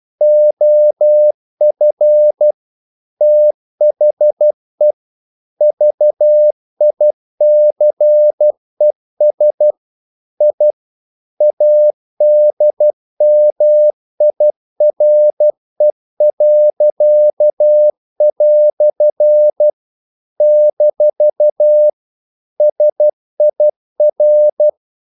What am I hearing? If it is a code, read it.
Morse code, heard as a keyed tone, about 12 wpm, 600 Hz. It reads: OF THE VICES I ADMIRE." - SIR
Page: English quotations in CW (telegraphy - morse)